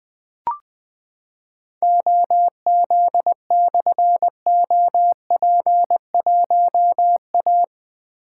Morse code, heard as a keyed tone, about 20 wpm, 700 Hz. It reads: OZ/OP1A
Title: Difficult Call Signs 20wpm